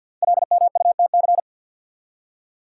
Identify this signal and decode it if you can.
Morse code, heard as a keyed tone, about 53 wpm, 700 Hz. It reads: LMUTC
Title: Code Groups - 5 Letters - 50wpm